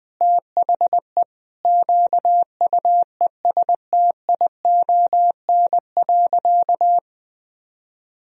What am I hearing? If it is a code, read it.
Morse code, heard as a keyed tone, about 21 wpm, 700 Hz. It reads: THE QUESTION.
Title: Sentences from Top 1000 Words - Review Part 4 - 20wpm